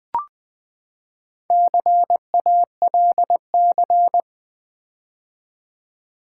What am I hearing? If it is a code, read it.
Morse code, heard as a keyed tone, about 20 wpm, 700 Hz. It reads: CALC
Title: Build A Word - 4 Letters At A Time - 11-Letter Words - 20wpm